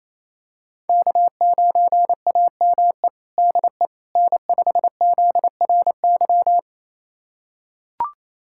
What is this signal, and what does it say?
Morse code, heard as a keyed tone, about 28 wpm, 700 Hz. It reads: K9AME DE N5ZRY
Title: USA Callsign DE Callsign - 5 Characters - 28wpm